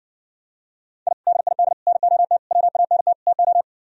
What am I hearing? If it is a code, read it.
Morse code, heard as a keyed tone, about 70 wpm, 700 Hz. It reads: I DID NOT WANT TO